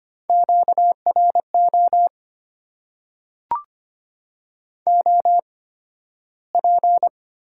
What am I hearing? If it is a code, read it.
Morse code, heard as a keyed tone, about 25 wpm, spaced as for 13 wpm, 690 Hz. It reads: QRO OP
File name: Top 90 QSO Elements - 8x Character Spacing - 25wpm